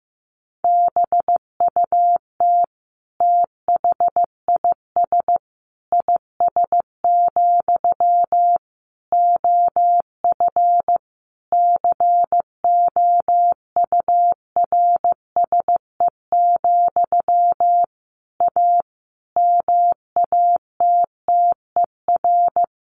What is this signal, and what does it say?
Morse code, heard as a keyed tone, about 14 wpm, 700 Hz. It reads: BUT THIS IS, OF COURSE, A MATTER
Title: Sentences from Top 1500 Words 15wpm